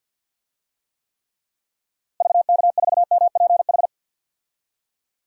Morse code, heard as a keyed tone, about 52 wpm, 700 Hz. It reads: VK4MPH
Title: DXCC-100 Call Signs - 6 Characters - Repeated 3x - 50wpm